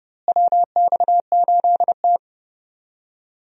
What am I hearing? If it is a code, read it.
Morse code, heard as a keyed tone, about 31 wpm, 700 Hz. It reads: WX8T
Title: USA Call Signs - 4 Characters - Repeated 3x - 30wpm